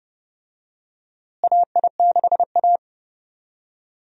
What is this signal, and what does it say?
Morse code, heard as a keyed tone, about 30 wpm, 700 Hz. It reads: AI6A
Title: USA Call Signs - 4 Characters - Repeated 2x - 30wpm